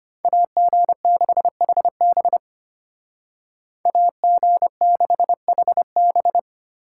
Morse code, heard as a keyed tone, about 33 wpm, 700 Hz. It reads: AG6HB AG6HB